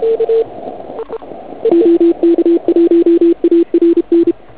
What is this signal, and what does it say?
Morse code, heard as a keyed tone, about 31 wpm, 340 Hz. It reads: OK1ARN
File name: Archiv / CUC106 / cuc106 PILEUP